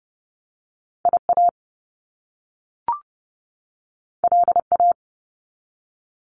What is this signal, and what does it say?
Morse code, heard as a keyed tone, about 30 wpm, 700 Hz. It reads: IA LA
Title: US States 30wpm